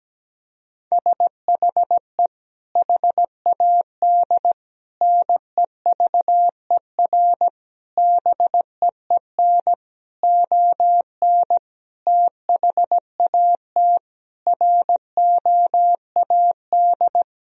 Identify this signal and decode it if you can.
Morse code, heard as a keyed tone, about 17 wpm, 700 Hz. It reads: SHE HAD NEVER BEEN ON THAT ROAD
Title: Sentences from Top 1000 Words - Review Part 2 - 17wpm